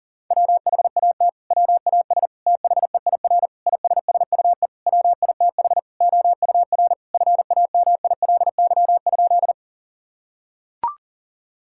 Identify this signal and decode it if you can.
Morse code, heard as a keyed tone, about 40 wpm, 700 Hz. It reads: WHAT WAS THEIR ISSUE WITH OUR FAMILY?
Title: Sentences from Top 1000 Words - Review Part 6 - 40wpm